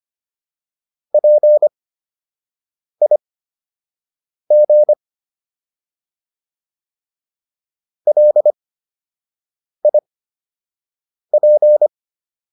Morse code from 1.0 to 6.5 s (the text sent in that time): PIG